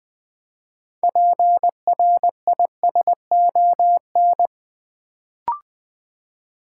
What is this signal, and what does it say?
Morse code, heard as a keyed tone, about 20 wpm, 700 Hz. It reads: PRISON
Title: Build A Word - 3 Letters At A Time - 6-Letter Words - 20wpm